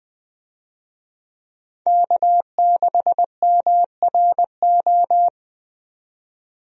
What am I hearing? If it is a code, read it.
Morse code, heard as a keyed tone, about 20 wpm, 700 Hz. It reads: K6MRO
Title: USA Call Signs - 5 Characters - 20wpm